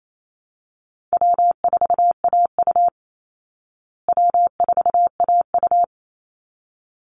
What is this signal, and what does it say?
Morse code, heard as a keyed tone, about 28 wpm, 700 Hz. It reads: W4AU W4AU
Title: Easy Call Signs Repeated 2x - 28wpm